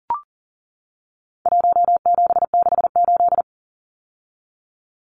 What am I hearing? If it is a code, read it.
Morse code, heard as a keyed tone, about 40 wpm, 700 Hz. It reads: 1768
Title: Numbers - Contest Style - 4-Digit 40wpm